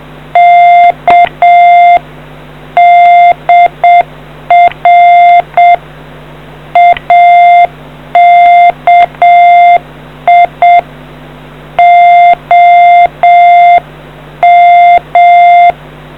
Morse code, heard as a keyed tone, about 7 wpm, 2100 Hz. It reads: K DR AKI OM